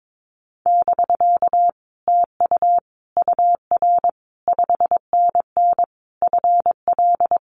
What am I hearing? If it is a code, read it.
Morse code, heard as a keyed tone, about 22 wpm, 700 Hz. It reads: <BK> TU UR 5NN FL
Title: Common QSO Phrases Encore 22wpm